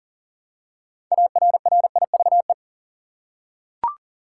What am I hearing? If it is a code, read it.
Morse code, heard as a keyed tone, about 40 wpm, 700 Hz. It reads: ARRIVE
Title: Six Letter Words - 40wpm